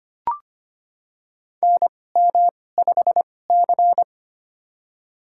Morse code, heard as a keyed tone, about 25 wpm, spaced as for 18 wpm, 700 Hz. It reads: NM5C
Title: USA Call Signs - 2x Character Spacing - 25wpm